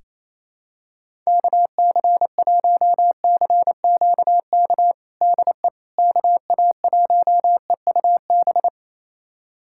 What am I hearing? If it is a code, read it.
Morse code, heard as a keyed tone, about 27 wpm, 700 Hz. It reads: KC1CQK DE KA1EUB
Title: USA Callsign DE Callsign 28wpm